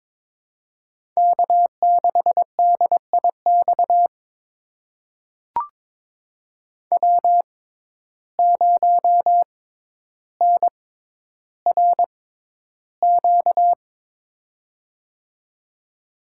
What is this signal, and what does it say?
Morse code, heard as a keyed tone, about 22 wpm, spaced as for 11 wpm, 700 Hz. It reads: K6DIX W0NRQ
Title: USA Call Signs - 6x Character Spacing - 22wpm